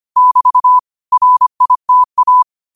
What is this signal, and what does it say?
Morse code, heard as a keyed tone, about 25 wpm, 1000 Hz. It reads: <BT> RITA